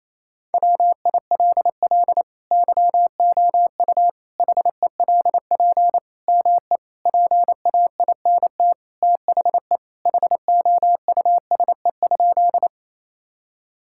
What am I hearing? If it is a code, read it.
Morse code, heard as a keyed tone, about 28 wpm, 700 Hz. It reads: WILL YOU HELP ME PAINT THE HOUSE?
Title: Sentences from Top 1000 Words - Review Part 6 - 28wpm